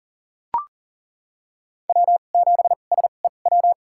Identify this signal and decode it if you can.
Morse code, heard as a keyed tone, about 40 wpm, spaced as for 29 wpm, 700 Hz. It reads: W7SEW
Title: USA Call Signs - 5 Characters - 2x Character Spacing - 40wpm